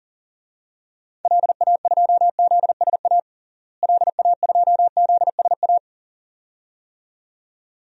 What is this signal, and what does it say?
Morse code, heard as a keyed tone, about 40 wpm, 700 Hz. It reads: LA2ZSA LA2ZSA
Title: DXCC-100 Call Signs - 6 Characters - Repeated 2x - 40wpm